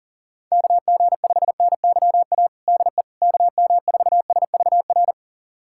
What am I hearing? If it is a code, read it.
Morse code, heard as a keyed tone, about 37 wpm, 700 Hz. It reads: KG5NYA DE KM4SVR
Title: USA Callsign DE Callsign - 6 Characters - 40wpm